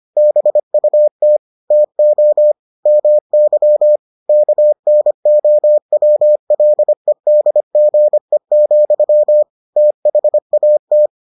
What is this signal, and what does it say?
Morse code, heard as a keyed tone, about 25 wpm, 600 Hz. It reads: BUT TO MY KNOWLEDGE, THAT